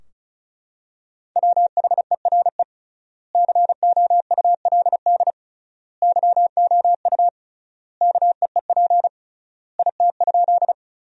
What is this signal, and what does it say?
Morse code, heard as a keyed tone, about 34 wpm, 700 Hz. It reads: WHERE COULD YOU KEEP IT?
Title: 3x Word-Spacing - Sentences from Top 300 Words - 35wpm